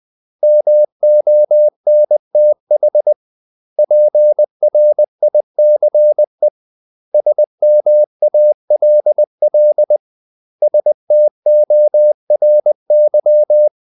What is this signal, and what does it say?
Morse code, heard as a keyed tone, about 20 wpm, 600 Hz. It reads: MONTH PRICE SMALL STORY
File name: CW / Practice / 500MostCommonEnglishWords20